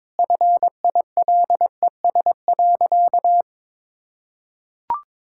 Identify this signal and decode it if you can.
Morse code, heard as a keyed tone, about 22 wpm, 700 Hz. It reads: FILES.
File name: practice / 22wpm / Easy Sentences with Hints - 22wpm